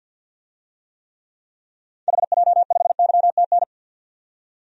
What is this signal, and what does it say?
Morse code, heard as a keyed tone, about 46 wpm, 700 Hz. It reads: HJ5YTN
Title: DXCC-100 Call Signs - 6 Characters - 50wpm